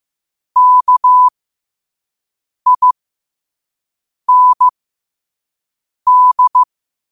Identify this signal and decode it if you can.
Morse code, heard as a keyed tone, about 15 wpm, spaced as for 5 wpm, 1000 Hz. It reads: KIND
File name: qotdfiles / 5wpm